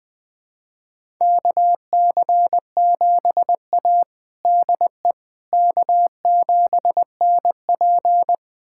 Morse code, heard as a keyed tone, about 20 wpm, 700 Hz. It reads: KC7A DE K7NP
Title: USA Callsign DE Callsign - 4 Characters - 20wpm